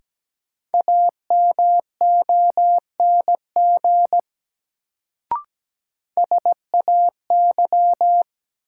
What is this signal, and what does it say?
Morse code, heard as a keyed tone, about 17 wpm, 700 Hz. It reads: AMONG SAY
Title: Sets of 2 Words - New Word Drill - 17wpm